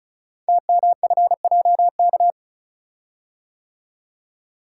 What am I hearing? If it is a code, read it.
Morse code, heard as a keyed tone, about 36 wpm, 700 Hz. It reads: TMFJK
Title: Code Groups - 5 Letter:Numbers - 35wpm